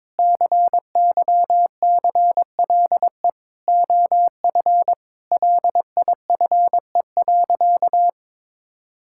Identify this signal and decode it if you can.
Morse code, heard as a keyed tone, about 22 wpm, 700 Hz. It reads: CYCLE OF LIFE.
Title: Sentences from Top 1000 Words - Review Part 6 - 22wpm